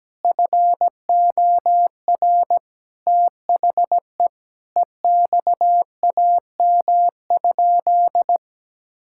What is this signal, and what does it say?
Morse code, heard as a keyed tone, about 17 wpm, 700 Hz. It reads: FOR THE EXAM?
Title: Sentences from Top 1000 Words - Review Part 5 - 17wpm